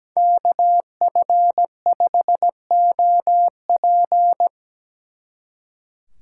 Morse code, heard as a keyed tone, about 20 wpm, 700 Hz. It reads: KF5OP
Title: Speed Racing - USA 2x2 Call Signs 17wpm